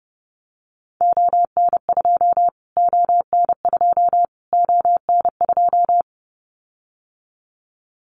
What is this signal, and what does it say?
Morse code, heard as a keyed tone, about 30 wpm, 700 Hz. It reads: ON2 ON2 ON2
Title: Callsign Prefixes for Top 25 DXCC - Repeated 3x - 30wpm